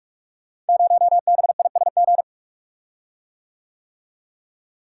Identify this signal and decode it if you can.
Morse code, heard as a keyed tone, about 44 wpm, 700 Hz. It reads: 0BISG
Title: Code Groups - 5 Letter:Numbers - 45wpm